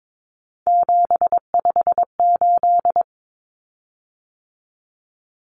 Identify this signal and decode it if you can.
Morse code, heard as a keyed tone, about 22 wpm, 700 Hz. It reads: 758
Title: Common Cut Numbers - 3-Digit 22wpm